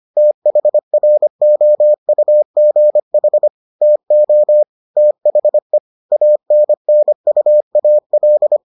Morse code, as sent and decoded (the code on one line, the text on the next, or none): - .... .-. --- ..- --. .... / - --- / - .... / .- -. -. ..- .- .-..
THROUGH TO THE ANNUAL